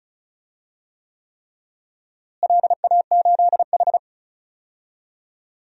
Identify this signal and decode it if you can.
Morse code, heard as a keyed tone, about 34 wpm, 700 Hz. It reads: LA8H
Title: DXCC-100 Call Signs - 4 Characters - Repeated 3x - 35wpm